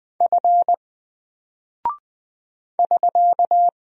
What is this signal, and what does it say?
Morse code, heard as a keyed tone, about 20 wpm, 700 Hz. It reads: F <SK>
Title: Single Character (Letter:Number:Punctuation:Prosign) 20wpm